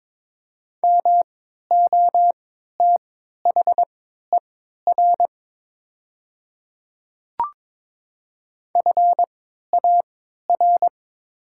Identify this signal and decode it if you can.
Morse code, heard as a keyed tone, about 22 wpm, spaced as for 12 wpm, 700 Hz. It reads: MOTHER FAR
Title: 3x Character Spacing - Top 300 Words - 22wpm